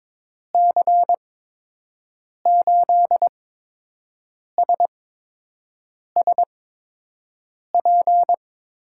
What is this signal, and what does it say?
Morse code, heard as a keyed tone, about 22 wpm, spaced as for 6 wpm, 700 Hz. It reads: C8SSP
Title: USA Call Signs - 6 Characters - 8x Character Spacing - 22wpm